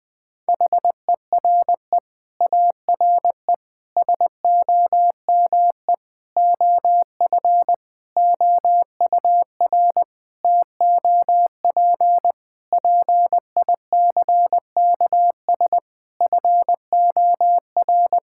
Morse code, heard as a keyed tone, about 21 wpm, 700 Hz. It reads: HERE ARE SOME OF OUR TOP PICKS FOR
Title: Sets of 9 Words - 20wpm